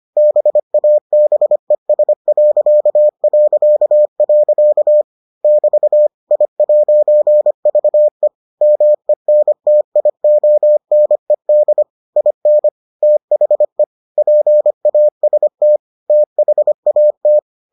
Morse code, heard as a keyed tone, about 25 wpm, 600 Hz. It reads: BABES... <BT> I'VE MENTIONED IN THE PAST THAT